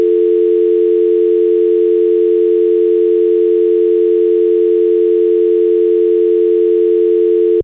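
Dial tone (North America), 350 + 440 Hz.
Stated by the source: Internal telephone dial tone sound effect.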